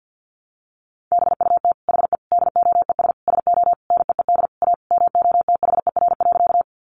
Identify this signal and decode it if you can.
Morse code, heard as a keyed tone, about 50 wpm, 700 Hz. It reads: BUT HE DOES SO NEED A MOTHER.
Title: Sentences from Top 200 Words 50wpm